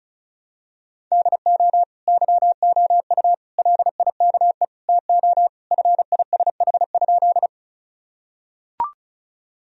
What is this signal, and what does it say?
Morse code, heard as a keyed tone, about 34 wpm, 700 Hz. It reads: DO YOU LIKE TO FISH?
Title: Sentences from Top 1000 Words - Review Part 1 - 35wpm